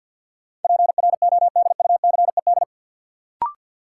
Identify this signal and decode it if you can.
Morse code, heard as a keyed tone, about 53 wpm, 700 Hz. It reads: PRODUCED
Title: Sentences from Top 1000 Words - Review Part 6 - New Word Drill - 50wpm